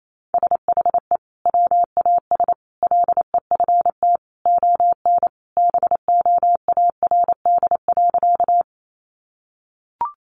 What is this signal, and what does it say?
Morse code, heard as a keyed tone, about 28 wpm, 700 Hz. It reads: SHE WAS LEFT ON BOARD.
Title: Sentences from Top 700 Words 28wpm